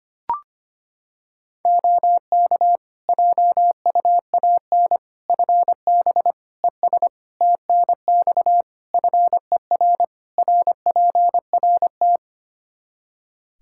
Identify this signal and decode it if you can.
Morse code, heard as a keyed tone, about 25 wpm, 700 Hz. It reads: OK JUAN FB ES TNX FER RPRT
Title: K1USN SST QSO Practice 25wpm